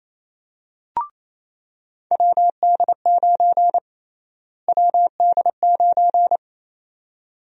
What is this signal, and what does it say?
Morse code, heard as a keyed tone, about 28 wpm, 700 Hz. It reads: WD9 WD9
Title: USA Call Sign Prefixes - Repeated 2x - 28wpm